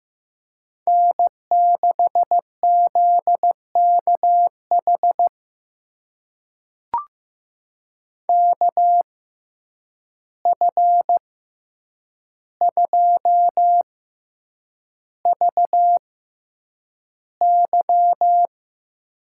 Morse code, heard as a keyed tone, about 15 wpm, spaced as for 8 wpm, 700 Hz. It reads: N6ZKH KF2VY
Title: USA Call Signs - 5 Characters - 6x Character Spacing - 15wpm